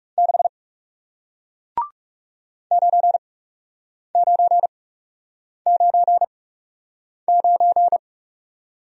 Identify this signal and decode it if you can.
Morse code, heard as a keyed tone, about 41 wpm, spaced as for 8 wpm, 700 Hz. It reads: 6 9999